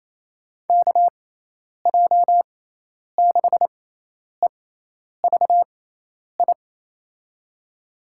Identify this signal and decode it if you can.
Morse code, heard as a keyed tone, about 28 wpm, spaced as for 10 wpm, 700 Hz. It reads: KJ6EVI
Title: USA Call Signs - 6 Characters - 6x Character Spacing - 28wpm